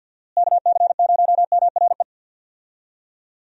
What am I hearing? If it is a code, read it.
Morse code, heard as a keyed tone, about 52 wpm, 700 Hz. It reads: KC0MRE
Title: USA Call Signs - 6 Characters - Repeated 2x - 50wpm